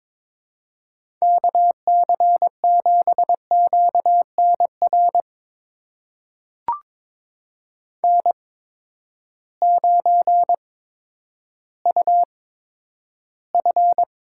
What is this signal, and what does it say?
Morse code, heard as a keyed tone, about 22 wpm, spaced as for 11 wpm, 700 Hz. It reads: KC7QNR N9UF
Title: USA Call Signs - 8x Character Spacing - 22wpm